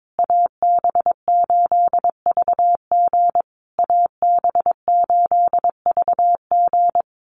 Morse code, heard as a keyed tone, about 22 wpm, 700 Hz. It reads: AB8VG AB8VG
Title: USA Call Signs Repeated 3x 22wpm